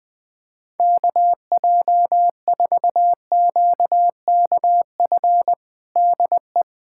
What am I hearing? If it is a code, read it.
Morse code, heard as a keyed tone, about 20 wpm, 700 Hz. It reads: KJ4QKF DE
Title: USA Callsign DE Callsign - 6 Characters - 20wpm